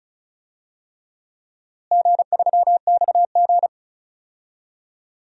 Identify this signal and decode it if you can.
Morse code, heard as a keyed tone, about 36 wpm, 700 Hz. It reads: G3XG
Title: DXCC-100 Call Signs - 4 Characters - Repeated 3x - 35wpm